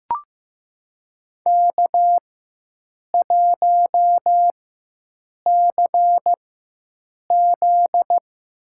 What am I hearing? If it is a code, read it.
Morse code, heard as a keyed tone, about 15 wpm, spaced as for 7 wpm, 700 Hz. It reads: K1CZ
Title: USA Call Signs - 4x Character Spacing - 15wpm